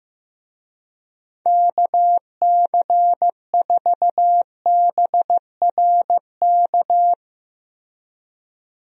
Morse code, heard as a keyed tone, about 15 wpm, 700 Hz. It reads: KC4BRK
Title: USA Call Signs - 6 Characters - Repeated 2x - 15wpm